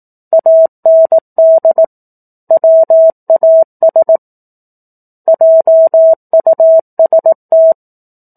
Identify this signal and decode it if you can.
Morse code, heard as a keyed tone, about 18 wpm, 650 Hz. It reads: AND WAS JUST
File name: 18 WPM